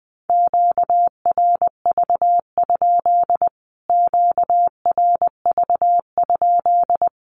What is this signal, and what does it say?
Morse code, heard as a keyed tone, about 20 wpm, 700 Hz. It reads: QRV? QRV?
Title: Q-Codes Repeated 3x 20wpm